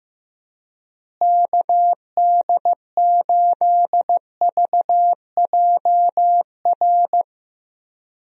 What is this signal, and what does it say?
Morse code, heard as a keyed tone, about 15 wpm, 700 Hz. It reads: KD8VJR
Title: USA Call Signs - 6 Characters - Repeated 2x - 15wpm